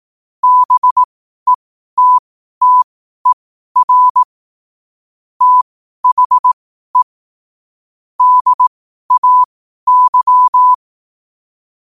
Morse code, heard as a keyed tone, about 18 wpm, spaced as for 12 wpm, 1000 Hz. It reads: BETTER THE DAY
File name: qotdfiles / 12wpm